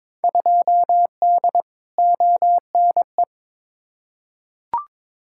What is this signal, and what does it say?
Morse code, heard as a keyed tone, about 22 wpm, 700 Hz. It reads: 2D ONE
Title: Field Day Exchanges - 22wpm